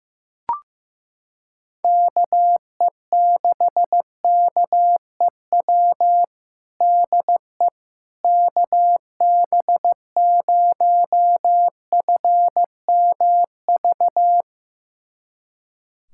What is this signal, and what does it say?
Morse code, heard as a keyed tone, about 15 wpm, 700 Hz. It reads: KE6KEW DE KB0FMV
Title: USA Callsign DE Callsign 15wpm